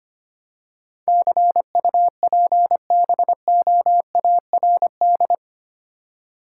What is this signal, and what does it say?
Morse code, heard as a keyed tone, about 25 wpm, 700 Hz. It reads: CUPBOARD
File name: 8 Letter Words - First 3 Letters Given - 25wpm